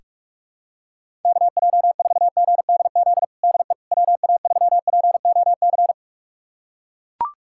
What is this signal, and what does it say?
Morse code, heard as a keyed tone, about 41 wpm, 700 Hz. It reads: KJ4GDZ DE WA3POC
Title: USA Callsign DE Callsign 45wpm